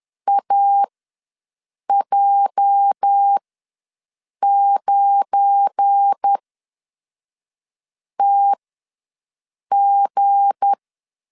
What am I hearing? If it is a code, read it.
Morse code, heard as a keyed tone, about 10 wpm, spaced as for 5 wpm, 800 Hz. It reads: AJ9TG
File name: LEZIONI / 8 YJ45 / 8 YJ ws 10wpm 4043